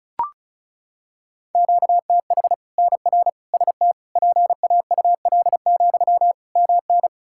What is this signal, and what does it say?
Morse code, heard as a keyed tone, about 36 wpm, 700 Hz. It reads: QTH NR ST PAUL, MN
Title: K1USN SST QSO Practice 35wpm